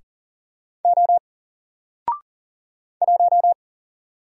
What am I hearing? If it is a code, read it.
Morse code, heard as a keyed tone, about 40 wpm, 700 Hz. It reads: O 1
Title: Single Letter-Number 40wpm